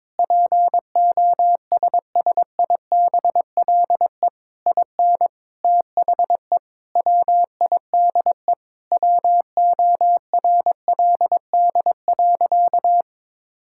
Morse code, heard as a keyed tone, about 22 wpm, 700 Hz. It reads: POSSIBLE IN THE WIDE WORLD.
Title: Sentences from Top 1000 Words - Review Part 6 - 22wpm